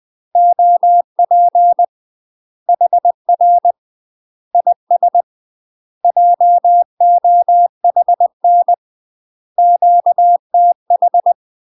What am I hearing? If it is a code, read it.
Morse code, heard as a keyed tone, about 20 wpm, 700 Hz. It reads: OP HR IS JOHN QTH
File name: UK MASTER QSOs TRAINING / 20WPM / UK MASTER QSOs 0015